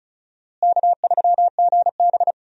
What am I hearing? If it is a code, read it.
Morse code, heard as a keyed tone, about 36 wpm, 700 Hz. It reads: K3GB
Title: USA Call Signs - 4 Characters - Repeated 3x - 35wpm